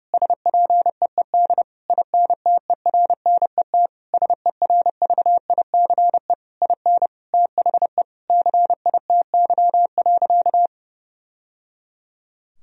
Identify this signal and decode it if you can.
Morse code, heard as a keyed tone, about 30 wpm, 700 Hz. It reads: SPEED INTERNET SERVICE IN THE CITY.
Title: Sentences from Top 1000 Words - Review Part 4 - 30wpm